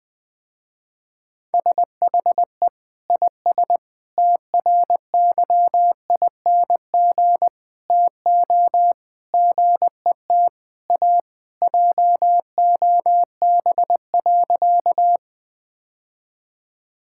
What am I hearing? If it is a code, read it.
Morse code, heard as a keyed tone, about 20 wpm, 700 Hz. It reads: SHE IS TRYING TO GET A JOB.
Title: Sentences with 3-Word Phrases - Spoken Beginning - 20wpm